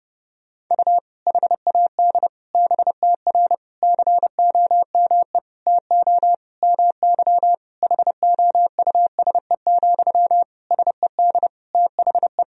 Morse code, heard as a keyed tone, about 31 wpm, 700 Hz. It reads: U HAD BTR COME TO MY HOUSE, SED THE
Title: Sentences from Top 400 Words and Abbreviations 30wpm